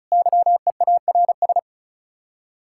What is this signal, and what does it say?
Morse code, heard as a keyed tone, about 44 wpm, 700 Hz. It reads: YEARS